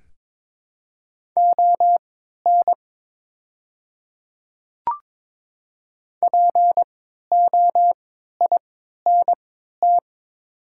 Morse code, heard as a keyed tone, about 22 wpm, spaced as for 13 wpm, 690 Hz. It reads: ON POINT
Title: 3x Character Spacing - Top 200 Words - 22wpm